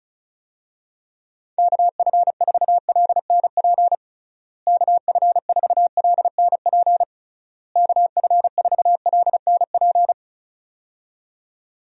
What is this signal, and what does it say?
Morse code, heard as a keyed tone, about 36 wpm, 700 Hz. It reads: KF4LNP KF4LNP KF4LNP
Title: USA Call Signs - 6 Characters - Repeated 3x - 35wpm